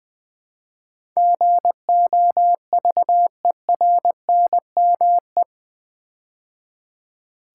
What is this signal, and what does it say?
Morse code, heard as a keyed tone, about 21 wpm, 700 Hz. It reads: GOVERNME
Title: Build A Word - 4 Letters At A Time - 10-Letter Words - 20wpm